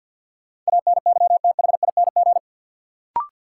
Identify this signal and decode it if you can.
Morse code, heard as a keyed tone, about 53 wpm, 700 Hz. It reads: ANYTHING
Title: Sentences from Top 1000 Words - New Word Drill - 50wpm